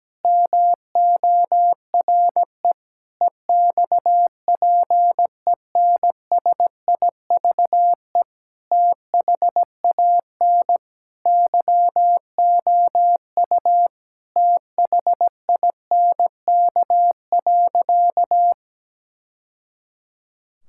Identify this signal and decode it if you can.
Morse code, heard as a keyed tone, about 17 wpm, 700 Hz. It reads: MORE EXPENSIVE THAN YOU THINK.
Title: Sentences from Top 1000 Words - Review Part 6 - 17wpm